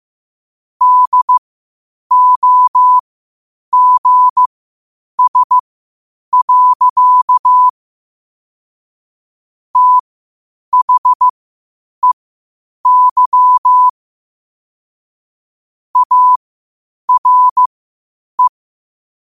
Morse code, heard as a keyed tone, about 15 wpm, spaced as for 8 wpm, 1000 Hz. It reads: DOGS. THEY ARE
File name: qotdfiles / 8wpm